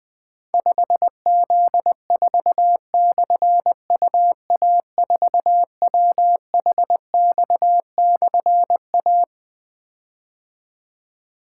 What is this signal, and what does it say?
Morse code, heard as a keyed tone, about 21 wpm, 700 Hz. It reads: 5Z4/UA4WHX/A
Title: Difficult Call Signs 20wpm